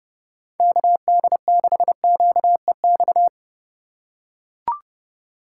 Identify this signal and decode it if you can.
Morse code, heard as a keyed tone, about 30 wpm, 700 Hz. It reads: KD6QEX
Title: USA Call Signs - 6 Characters - 4x Character Spacing - 30wpm